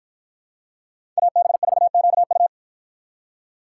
Morse code, heard as a keyed tone, about 53 wpm, 700 Hz. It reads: AB4QU